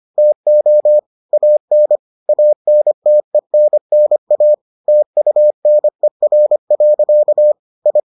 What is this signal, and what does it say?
Morse code, heard as a keyed tone, about 25 wpm, 600 Hz. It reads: TO AN ANTENNA TUNER. I